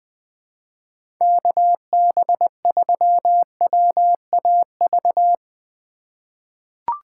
Morse code, heard as a keyed tone, about 20 wpm, 700 Hz. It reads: KB3WAV
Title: 2023 POTA Activator Call Signs - 20wpm